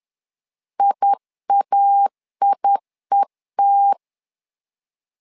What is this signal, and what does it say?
Morse code, heard as a keyed tone, about 10 wpm, 800 Hz. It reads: IAIET
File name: LEZIONI / 2 UHDN / 2 UH 10wpm 1714